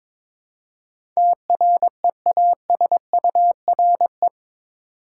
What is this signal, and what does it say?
Morse code, heard as a keyed tone, about 22 wpm, 700 Hz. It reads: TREASURE
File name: practice / 22wpm / Eight Letter Words - 22wpm